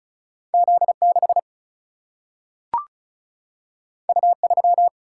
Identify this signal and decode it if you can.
Morse code, heard as a keyed tone, about 36 wpm, 700 Hz. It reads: Z 6 U 3
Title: Single-Letter Single-Number 35wpm